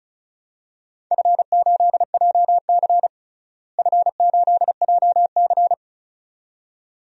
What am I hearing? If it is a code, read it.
Morse code, heard as a keyed tone, about 35 wpm, 700 Hz. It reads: F8JC F8JC
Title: DXCC-100 Call Signs - 4 Characters - Repeated 2x - 35wpm